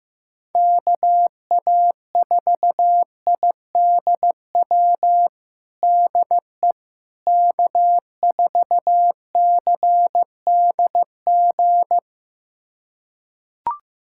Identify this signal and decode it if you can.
Morse code, heard as a keyed tone, about 15 wpm, 700 Hz. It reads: KA4IDW DE K4CDG
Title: USA Callsign DE Callsign 15wpm